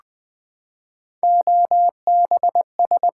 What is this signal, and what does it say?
Morse code, heard as a keyed tone, about 20 wpm, 700 Hz. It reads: OBS
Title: Build A Word - 3 Letters At A Time - 9-Letter Words - 20wpm